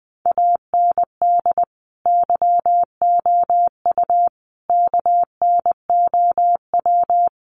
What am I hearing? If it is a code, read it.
Morse code, heard as a keyed tone, about 20 wpm, 700 Hz. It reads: AND YOU KNOW
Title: Sentences from Top 1200 Words 20wpm